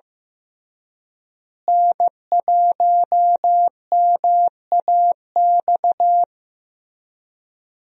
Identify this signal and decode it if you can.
Morse code, heard as a keyed tone, about 16 wpm, 700 Hz. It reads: N1MAX
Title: USA Call Signs - 5 Characters - Repeated 3x - 15wpm